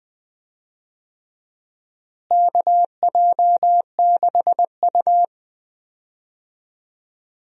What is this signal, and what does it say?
Morse code, heard as a keyed tone, about 20 wpm, 700 Hz. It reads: KJ6U
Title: USA Call Signs - 4 Characters - Repeated 2x - 20wpm